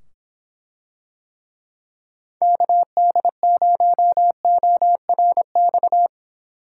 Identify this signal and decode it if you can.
Morse code, heard as a keyed tone, about 25 wpm, 700 Hz. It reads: KD0ORX